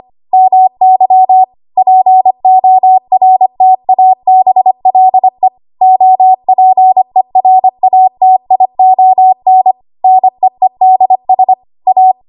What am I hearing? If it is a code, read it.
Morse code, heard as a keyed tone, about 25 wpm, 750 Hz. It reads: MY PORTABLE OPERATION NEEDS A
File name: Archive / 25 WPM / 25WPM